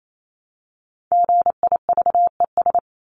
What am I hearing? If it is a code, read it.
Morse code, heard as a keyed tone, about 28 wpm, 700 Hz. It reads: GIVES
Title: ES:S Suffix - Top 100 Words with Base Words 28wpm